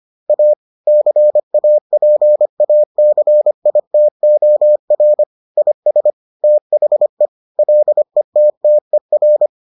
Morse code, heard as a keyed tone, about 25 wpm, 600 Hz. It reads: A CAPACITOR IS THE LETTER